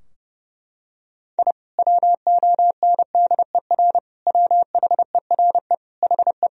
Morse code, heard as a keyed tone, about 30 wpm, 700 Hz. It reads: I WONDER WHERE HE
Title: Sentences from Top 1000 Words - Review Part 3 - 30wpm